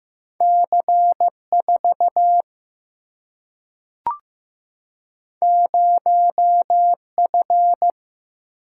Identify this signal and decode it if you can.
Morse code, heard as a keyed tone, about 15 wpm, 700 Hz. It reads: C 4 0 F
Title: Single-Letter Single-Number 15wpm